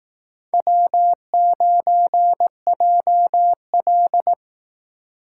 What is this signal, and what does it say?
Morse code, heard as a keyed tone, about 17 wpm, 700 Hz. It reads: W9JL
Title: Speed Racing - USA 1x2 Call Signs 15wpm